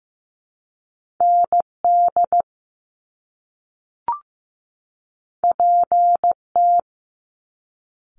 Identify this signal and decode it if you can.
Morse code, heard as a keyed tone, about 15 wpm, 700 Hz. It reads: ND PT
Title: Top Two Letter Combinations in Words 15wpm